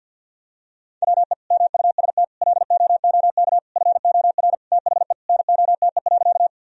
Morse code, heard as a keyed tone, about 50 wpm, 700 Hz. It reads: WE MUST LOOK FOR THE NOTE.
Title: Sentences from Top 1000 Words - Review Part 2 - 50wpm